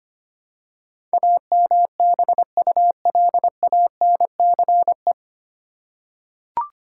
Morse code, heard as a keyed tone, about 25 wpm, 700 Hz. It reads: AMBULANCE
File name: Nine Letter Words - 25wpm